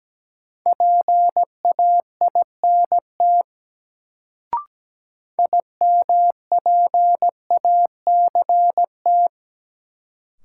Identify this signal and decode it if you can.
Morse code, heard as a keyed tone, about 17 wpm, 700 Hz. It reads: PAINT IMPACT
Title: Sentences from Top 1000 Words - Review Part 6 - New Word Drill - 17wpm